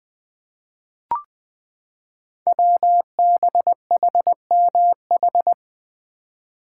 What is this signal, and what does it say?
Morse code, heard as a keyed tone, about 20 wpm, 700 Hz. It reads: WBHMH
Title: Code Groups - 5 Letter:Numbers - 20wpm